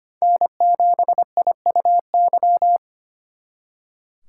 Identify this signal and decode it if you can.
Morse code, heard as a keyed tone, about 32 wpm, 700 Hz. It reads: N7IUY